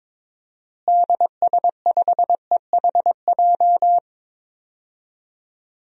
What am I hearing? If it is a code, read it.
Morse code, heard as a keyed tone, about 22 wpm, 700 Hz. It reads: DS5EHJ
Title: DXCC-100 Call Signs - 6 Characters - Repeated 2x - 22wpm